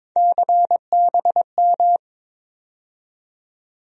Morse code, heard as a keyed tone, about 22 wpm, 700 Hz. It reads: CBM
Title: USA Call Sign Postfixes - 3 Letters 22wpm